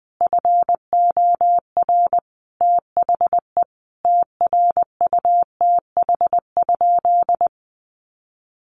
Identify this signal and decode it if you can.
Morse code, heard as a keyed tone, about 20 wpm, 700 Hz. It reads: FOR THE TRUTH?
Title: Sentences from Top 1300 Words 20wpm